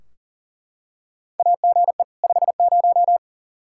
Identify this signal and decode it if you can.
Morse code, heard as a keyed tone, about 40 wpm, 700 Hz. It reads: AGE 50
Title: K1USN SST QSO Practice 40wpm